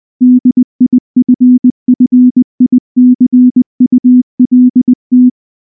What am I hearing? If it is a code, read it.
Morse code, heard as a keyed tone, about 19 wpm, 250 Hz. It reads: DIFFICULT
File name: morse / morse-training-20WPM-1